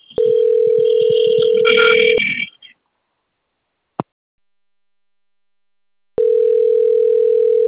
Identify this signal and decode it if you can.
Ringback tone (North America), 440 + 480 Hz.
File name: rg-1600-unknown- - - .1201